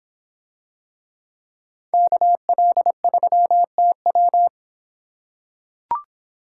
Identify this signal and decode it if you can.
Morse code, heard as a keyed tone, about 25 wpm, 700 Hz. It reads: KL3TW